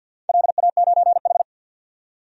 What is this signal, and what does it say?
Morse code, heard as a keyed tone, about 50 wpm, 700 Hz. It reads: LA9H
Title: DXCC-100 Call Signs - 4 Characters - 50wpm